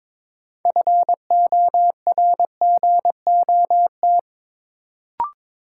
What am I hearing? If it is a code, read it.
Morse code, heard as a keyed tone, about 22 wpm, 700 Hz. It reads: FORGOT
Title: Sentences from Top 1400 Words - New Word Drill - 22wpm